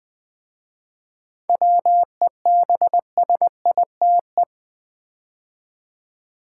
Morse code, heard as a keyed tone, about 20 wpm, 700 Hz. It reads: WEBSITE
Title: 7 Letter Words - First 3 Letters Given - 20wpm